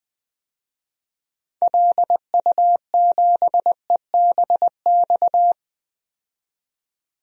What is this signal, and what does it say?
Morse code, heard as a keyed tone, about 20 wpm, 700 Hz. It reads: LU7EBX
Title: DXCC-100 Call Signs - 6 Characters - 20wpm